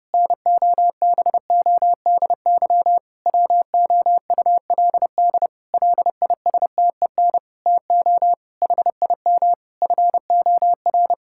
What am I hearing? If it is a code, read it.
Morse code, heard as a keyed tone, about 32 wpm, 700 Hz. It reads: NOBODY WOULD LISTEN TO HIM FOR
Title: Sentences from Top 1600 Words 30wpm